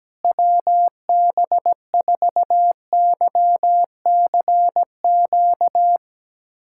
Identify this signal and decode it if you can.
Morse code, heard as a keyed tone, about 17 wpm, 700 Hz. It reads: WB4YCQ
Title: USA Callsign DE Callsign - 6 Characters - 17wpm